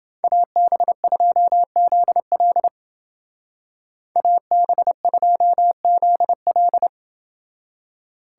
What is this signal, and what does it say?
Morse code, heard as a keyed tone, about 30 wpm, 700 Hz. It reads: AB2ZL AB2ZL